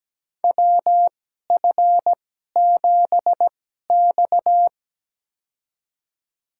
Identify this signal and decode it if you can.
Morse code, heard as a keyed tone, about 17 wpm, spaced as for 12 wpm, 700 Hz. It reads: WF7X
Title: USA Call Signs - 4 Characters - 2x Character Spacing - 17wpm